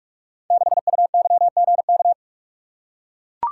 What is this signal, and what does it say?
Morse code, heard as a keyed tone, about 44 wpm, 690 Hz. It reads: 6UYGK
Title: Code Groups - 5 Letter:Numbers - 45wpm